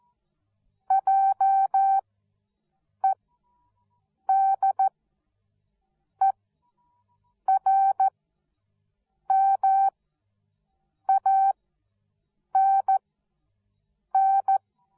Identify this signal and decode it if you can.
Morse code, heard as a keyed tone, about 14 wpm, spaced as for 6 wpm, 780 Hz. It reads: JEDERMANN